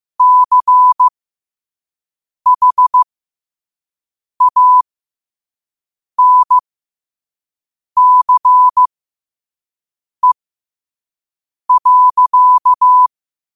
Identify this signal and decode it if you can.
Morse code, heard as a keyed tone, about 15 wpm, spaced as for 5 wpm, 1000 Hz. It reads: CHANCE.
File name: 5wpm